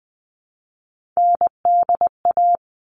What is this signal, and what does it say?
Morse code, heard as a keyed tone, about 20 wpm, 700 Hz. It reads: NDA
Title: All Cut Numbers - 3-Digit 20wpm